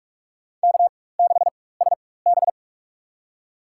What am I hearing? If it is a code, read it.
Morse code, heard as a keyed tone, about 46 wpm, spaced as for 21 wpm, 700 Hz. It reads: K6SB
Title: USA Call Signs - 4 Characters - 4x Character Spacing - 45wpm